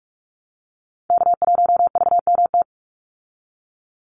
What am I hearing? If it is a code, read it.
Morse code, heard as a keyed tone, about 44 wpm, 700 Hz. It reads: K1VMT
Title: Call Signs Repeated 2x - 45wpm